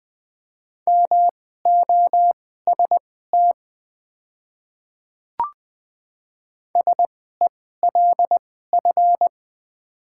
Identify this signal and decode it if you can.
Morse code, heard as a keyed tone, about 20 wpm, spaced as for 14 wpm, 700 Hz. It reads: MOST SELF
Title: 2x Character Spacing - Top 200 Words - 20wpm